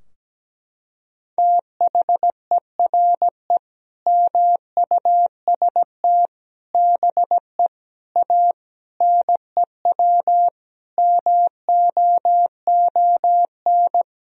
Morse code, heard as a keyed tone, about 17 wpm, 700 Hz. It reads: THERE MUST BE A NEW MOON
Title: Sentences from Top 1000 Words - Review Part 3 - 17wpm